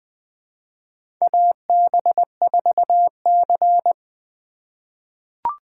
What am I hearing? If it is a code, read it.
Morse code, heard as a keyed tone, about 20 wpm, 700 Hz. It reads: AB4C
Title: USA Call Signs - 3x Character Spacing - 20wpm